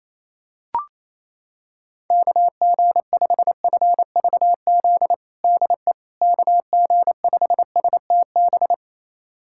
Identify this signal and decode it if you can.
Morse code, heard as a keyed tone, about 28 wpm, 700 Hz. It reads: KG5FVZ DE KG5STB
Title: USA Callsign DE Callsign - 6 Characters - 28wpm